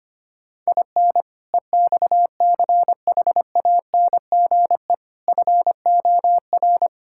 Morse code, heard as a keyed tone, about 25 wpm, 700 Hz. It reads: IN EXCHANGE FOR
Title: Sentences from Top 1000 Words - Review Part 6 - 25wpm